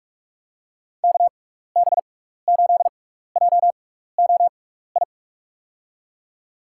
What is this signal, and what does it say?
Morse code, heard as a keyed tone, about 46 wpm, spaced as for 16 wpm, 690 Hz. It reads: KB8JOI
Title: USA Call Signs - 6x Character Spacing - 45wpm